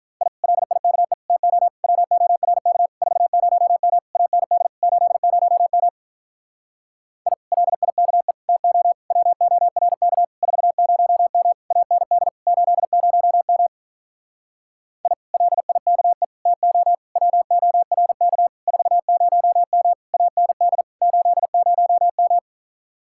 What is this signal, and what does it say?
Morse code, heard as a keyed tone, about 48 wpm, 700 Hz. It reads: I LIKE TO WORK 40M AND 80M I LIKE TO WORK 40M AND 80M I LIKE TO WORK 40M AND 80M